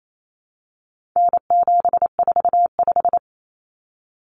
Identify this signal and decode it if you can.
Morse code, heard as a keyed tone, about 28 wpm, 700 Hz. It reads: N745
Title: Common Cut Numbers - 4-Digit 28wpm